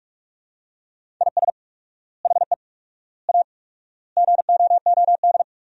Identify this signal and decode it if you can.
Morse code, heard as a keyed tone, about 45 wpm, 700 Hz. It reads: IS HE A GOOD
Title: 4x Word-Spacing - Sentences from Top 200 Words - 45wpm